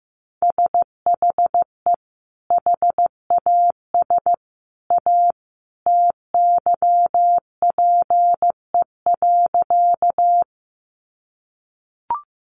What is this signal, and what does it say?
Morse code, heard as a keyed tone, about 14 wpm, 700 Hz. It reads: SHE HAS A TYPE.
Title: Sentences from Top 600 Words 15wpm